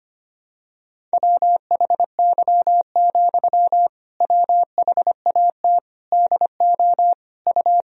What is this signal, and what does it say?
Morse code, heard as a keyed tone, about 25 wpm, 700 Hz. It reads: WHY, WHAT DO U
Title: Sentences from Top 200 Words and Abbreviations 25wpm